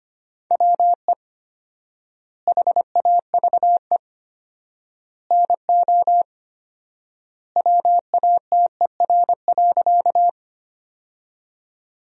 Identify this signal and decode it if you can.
Morse code, heard as a keyed tone, about 25 wpm, 700 Hz. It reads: WE HAVE NO WATER.
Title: 4x Word-Spacing - Sentences from Top 100 Words - 25wpm